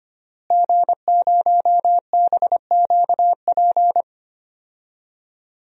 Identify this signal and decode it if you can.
Morse code, heard as a keyed tone, about 25 wpm, 700 Hz. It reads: G0BQP
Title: DXCC-100 Call Signs - 5 Characters - Repeated 3x - 25wpm